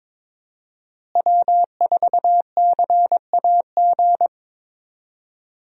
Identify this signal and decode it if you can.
Morse code, heard as a keyed tone, about 21 wpm, 700 Hz. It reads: W4CAG
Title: USA Call Signs - 5 Characters - Repeated 2x - 22wpm